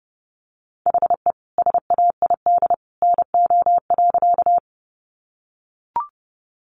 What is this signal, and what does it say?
Morse code, heard as a keyed tone, about 30 wpm, 700 Hz. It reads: HE SAID NO.
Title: Sentences from Top 100 Words 30wpm